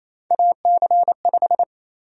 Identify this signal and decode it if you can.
Morse code, heard as a keyed tone, about 28 wpm, 700 Hz. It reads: AC5
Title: USA Call Sign Prefixes - 28wpm